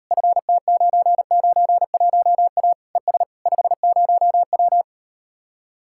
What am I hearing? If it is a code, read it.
Morse code, heard as a keyed tone, about 39 wpm, 700 Hz. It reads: FT991A ES 50W